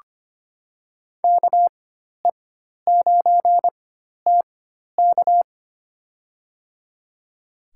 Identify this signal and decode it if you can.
Morse code, heard as a keyed tone, about 25 wpm, spaced as for 12 wpm, 700 Hz. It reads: KE9TK
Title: USA Call Signs - 5 Characters - 4x Character Spacing - 25wpm